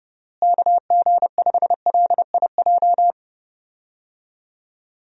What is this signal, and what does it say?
Morse code, heard as a keyed tone, about 30 wpm, 700 Hz. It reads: KG5LIJ
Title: USA Call Signs - 6 Characters - Repeated 3x - 30wpm